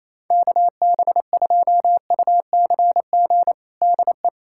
Morse code, heard as a keyed tone, about 28 wpm, 700 Hz. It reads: KB2UCG DE
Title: USA Callsign DE Callsign - 6 Characters - 28wpm